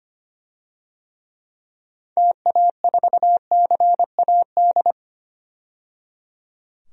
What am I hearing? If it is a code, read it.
Morse code, heard as a keyed tone, about 25 wpm, 700 Hz. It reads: TA4CAD
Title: DXCC-100 Call Signs - 6 Characters - 25wpm